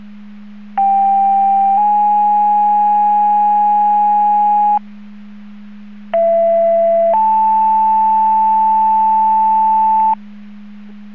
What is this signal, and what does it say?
Two-tone page, heard as a one-second tone, then a three-second tone, about 790 then 830 Hz, about 670 then 880 Hz. It is heard from 0.8 to 10.1 s.